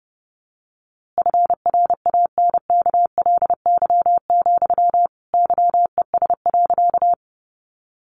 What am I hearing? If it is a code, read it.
Morse code, heard as a keyed tone, about 30 wpm, 700 Hz. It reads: FRANKLY, YES.
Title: A Sound Of Thunder 30wpm